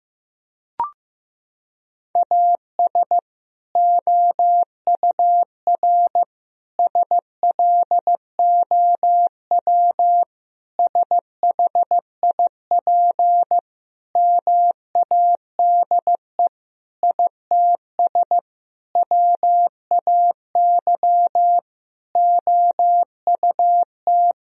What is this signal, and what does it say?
Morse code, heard as a keyed tone, about 16 wpm, 700 Hz. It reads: AS OUR SLOW SHIP MADE ITS WAY OUT
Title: Sentences from Top 1000 Words - Review Part 2 - 15wpm